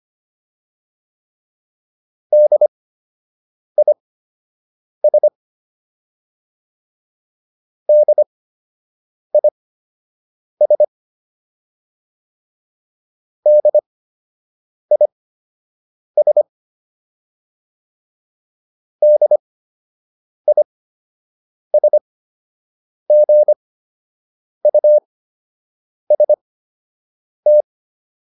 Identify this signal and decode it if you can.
Morse code, heard as a keyed tone, about 25 wpm, spaced as for 7 wpm, 600 Hz. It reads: DIS DIS DIS DISGUST
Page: fundamental-practice-files